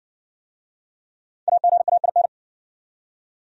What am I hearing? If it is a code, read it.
Morse code, heard as a keyed tone, about 58 wpm, 700 Hz. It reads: AGAIN